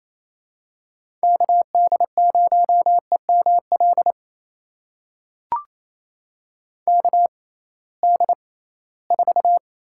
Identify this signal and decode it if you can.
Morse code, heard as a keyed tone, about 28 wpm, spaced as for 18 wpm, 700 Hz. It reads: KD0EML KD4
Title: USA Call Signs - 6 Characters - 6x Character Spacing - 28wpm